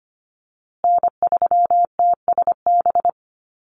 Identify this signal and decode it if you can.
Morse code, heard as a keyed tone, about 25 wpm, 700 Hz. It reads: N3TSB
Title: USA Call Signs Repeated 2x 25wpm